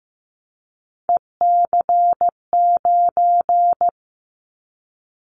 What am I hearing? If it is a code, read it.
Morse code, heard as a keyed tone, about 15 wpm, 700 Hz. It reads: EC9
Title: Callsign Prefixes for Top 100 DXCC - 15wpm